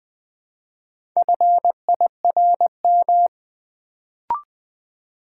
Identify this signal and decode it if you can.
Morse code, heard as a keyed tone, about 20 wpm, 700 Hz. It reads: FIRM
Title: Build A Word - 2 Letters At A Time - 4-Letter Words - 20wpm